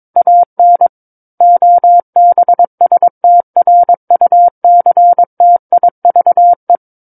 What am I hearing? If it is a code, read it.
Morse code, heard as a keyed tone, about 22 wpm, 700 Hz. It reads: AN OBSTRUCTIVE
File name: Book One / 13-Chapter 13 - 22 WPM